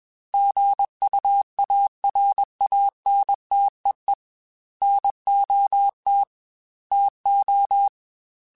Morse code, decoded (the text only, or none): GUARANTEE NOT TO